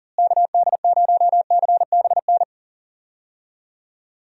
Morse code, heard as a keyed tone, about 40 wpm, 700 Hz. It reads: KD0CBN
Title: USA Call Signs - 6 Characters - Repeated 3x - 40wpm